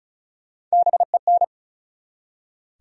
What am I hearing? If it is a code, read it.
Morse code, heard as a keyed tone, about 34 wpm, 700 Hz. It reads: BEN
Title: USA Common Names - 35wpm